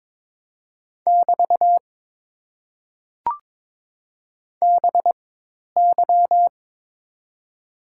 Morse code, heard as a keyed tone, about 22 wpm, spaced as for 10 wpm, 700 Hz. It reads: <BT> BY
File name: Top 90 QSO Elements - 4x Character Spacing - 22wpm